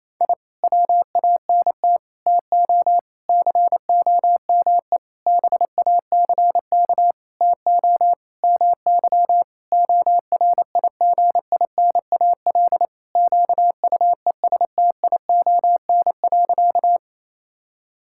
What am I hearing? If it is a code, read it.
Morse code, heard as a keyed tone, about 28 wpm, 700 Hz. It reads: I WANT TO COME BACK TO MY ORIGINAL QUESTION.
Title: Sentences from Top 1000 Words - Review Part 6 - 28wpm